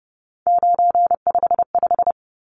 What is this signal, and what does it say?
Morse code, heard as a keyed tone, about 30 wpm, 700 Hz. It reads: 955
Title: Numbers - 3-Digit 30wpm